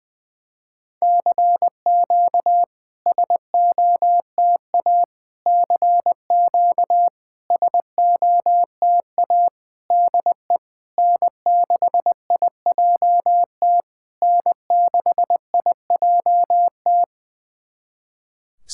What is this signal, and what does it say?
Morse code, heard as a keyed tone, about 20 wpm, 690 Hz. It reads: CQ SOTA CQ SOTA DE N6IJT N6IJT
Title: SOTA QSO Practice 20wpm